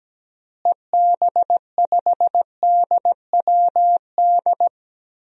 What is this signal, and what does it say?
Morse code, heard as a keyed tone, about 17 wpm, 700 Hz. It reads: EB5DWD
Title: DXCC-100 Call Signs - 6 Characters - Repeated 3x - 17wpm